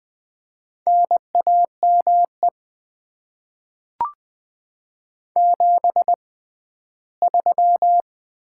Morse code, heard as a keyed tone, about 20 wpm, spaced as for 13 wpm, 700 Hz. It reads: NAME 73
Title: Top 20 QSO Elements - 6x Character Spacing - 20wpm